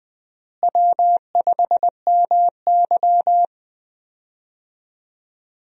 Morse code, heard as a keyed tone, about 20 wpm, 700 Hz. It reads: W5MY
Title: USA Call Signs - 4 Characters - Repeated 3x - 20wpm